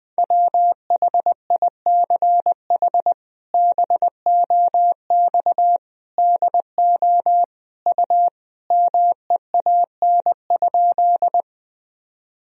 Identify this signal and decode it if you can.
Morse code, heard as a keyed tone, about 20 wpm, 700 Hz. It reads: WHICH BOX DO U MEAN?
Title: Sentences from Top 500 Words and Abbreviations 20wpm